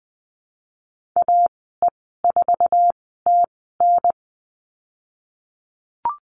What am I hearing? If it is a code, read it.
Morse code, heard as a keyed tone, about 20 wpm, spaced as for 14 wpm, 700 Hz. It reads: AE4TN
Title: 048 Lesson 9g - Number 4 - Call Signs 20wpm